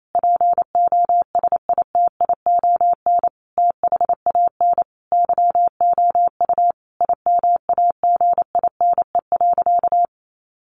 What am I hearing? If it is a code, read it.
Morse code, heard as a keyed tone, about 28 wpm, 700 Hz. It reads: POSITION THAN YOU IMAGINE.
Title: Sentences from Top 1400 Words 28wpm